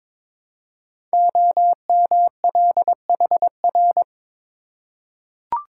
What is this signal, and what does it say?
Morse code, heard as a keyed tone, about 21 wpm, 700 Hz. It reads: OMLHR
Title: Code Groups - 5 Letters - 22wpm